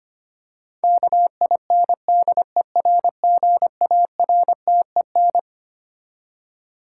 Morse code, heard as a keyed tone, about 25 wpm, 700 Hz. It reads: KINDERGARTEN
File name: Twelve Letter Words - 25wpm